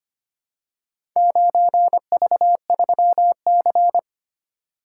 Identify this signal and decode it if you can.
Morse code, heard as a keyed tone, about 25 wpm, 700 Hz. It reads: 9V3C
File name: practice / 25wpm / Four Letter-Number Combinations - 25wpm